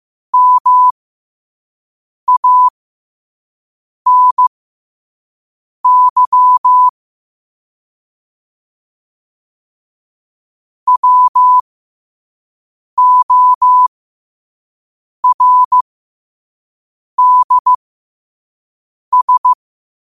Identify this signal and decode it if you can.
Morse code, heard as a keyed tone, about 15 wpm, spaced as for 5 wpm, 1000 Hz. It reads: MANY WORDS
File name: 5wpm